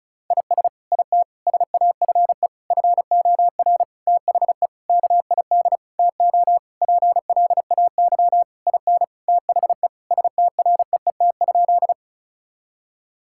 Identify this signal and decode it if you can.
Morse code, heard as a keyed tone, about 36 wpm, 700 Hz. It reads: IS IT SAFE FOR THE KID TO PLAY IN THE STREET?
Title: Sentences from Top 1600 Words 35wpm